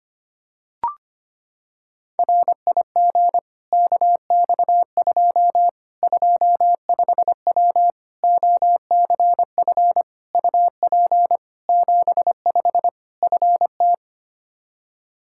Morse code, heard as a keyed tone, about 25 wpm, 700 Hz. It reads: RIG KX2 25W OCF UP 75 FT
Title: K1USN SST QSO Practice 25wpm